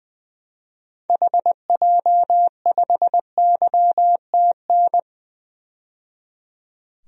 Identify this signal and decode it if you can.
Morse code, heard as a keyed tone, about 20 wpm, 700 Hz. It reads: HJ5YTN
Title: DXCC-100 Call Signs - 6 Characters - 20wpm